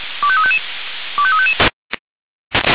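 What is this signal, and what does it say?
Five-tone selective call (ZVEI), heard as quick runs of five tones, tones of about 70 ms. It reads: 25355 25355
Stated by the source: ZVEI five-tone selective call received on SDR.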